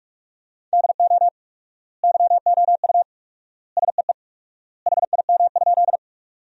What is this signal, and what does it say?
Morse code, heard as a keyed tone, about 46 wpm, 700 Hz. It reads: DO YOU SEE HIM?
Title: 4x Word-Spacing - Sentences from Top 100 Words - 45wpm